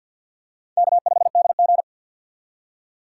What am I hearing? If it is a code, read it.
Morse code, heard as a keyed tone, about 53 wpm, 700 Hz. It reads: K5DG
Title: USA Call Signs - 6x Character Spacing - 50wpm